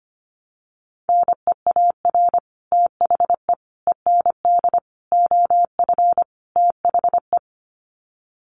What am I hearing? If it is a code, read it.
Morse code, heard as a keyed tone, about 25 wpm, 700 Hz. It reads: NEAR THE END OF THE
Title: Sets of 5 Words Encore 25wpm